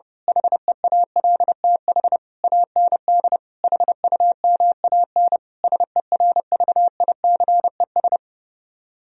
Morse code, heard as a keyed tone, about 29 wpm, 700 Hz. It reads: HEALTH AND HUMAN SERVICES
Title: Speed-Racing - Sets of 5 Words 30wpm